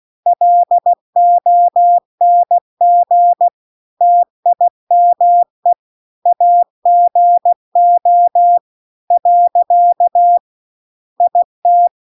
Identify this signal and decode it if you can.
Morse code, heard as a keyed tone, about 16 wpm, 700 Hz. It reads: LONG TIME AGO . IT